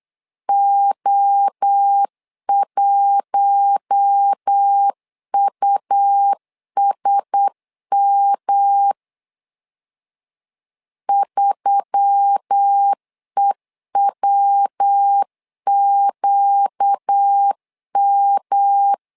Morse code, heard as a keyed tone, about 8 wpm, 800 Hz. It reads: O1USM 3EWQM
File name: LEZIONI / 4 Q1S3 8wpm 3037